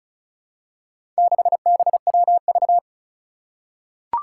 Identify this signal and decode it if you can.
Morse code, heard as a keyed tone, about 36 wpm, 700 Hz. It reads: 6BWV
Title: Four Challenging Letter-Number Combinations 35wpm